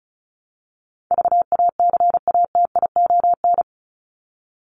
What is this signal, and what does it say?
Morse code, heard as a keyed tone, about 34 wpm, 700 Hz. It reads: VACATION
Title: Top 1501-1750 Words 35wpm